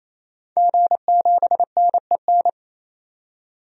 Morse code, heard as a keyed tone, about 28 wpm, 700 Hz. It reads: G7NEN
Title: DXCC-100 Call Signs - 5 Characters - 28wpm